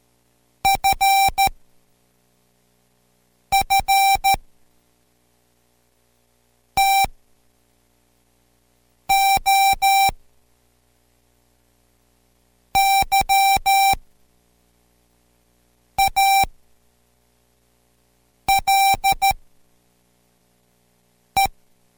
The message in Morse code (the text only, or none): FF TO YALE